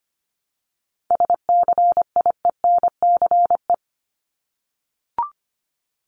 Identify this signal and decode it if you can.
Morse code, heard as a keyed tone, about 25 wpm, 700 Hz. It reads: SCIENCE
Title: Top 500 Words 25wpm